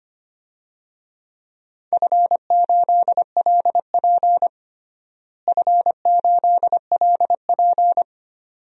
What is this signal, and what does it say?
Morse code, heard as a keyed tone, about 25 wpm, 700 Hz. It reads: F8LP F8LP
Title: DXCC-100 Call Signs - 4 Characters - Repeated 2x - 25wpm